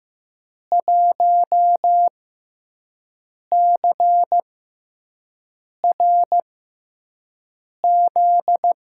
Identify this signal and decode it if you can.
Morse code, heard as a keyed tone, about 15 wpm, spaced as for 5 wpm, 700 Hz. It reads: 1CRZ
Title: USA Call Signs - 5 Characters - 6x Character Spacing - 15wpm